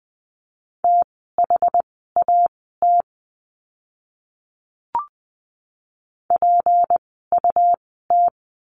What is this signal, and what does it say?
Morse code, heard as a keyed tone, about 20 wpm, spaced as for 14 wpm, 700 Hz. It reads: THAT PUT
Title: 192 Lesson 31f - Letter v - Words 20wpm